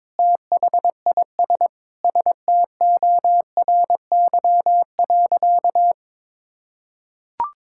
Morse code, heard as a keyed tone, about 22 wpm, 700 Hz. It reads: THIS STORY.
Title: Sentences from Top 1600 Words 22wpm